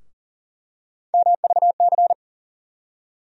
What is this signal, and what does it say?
Morse code, heard as a keyed tone, about 40 wpm, 700 Hz. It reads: MVC
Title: USA Call Sign Postfixes - 3 Letters 40wpm